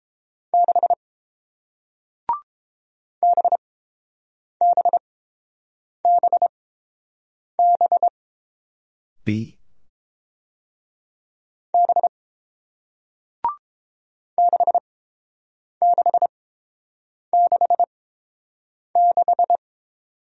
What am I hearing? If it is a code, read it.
Morse code, heard as a keyed tone, about 30 wpm, spaced as for 7 wpm, 700 Hz. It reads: 6 BBBB B 6666